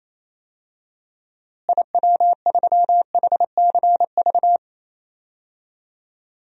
Morse code, heard as a keyed tone, about 28 wpm, 700 Hz. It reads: IW3HCV
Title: DXCC-100 Call Signs - 6 Characters - Repeated 2x - 28wpm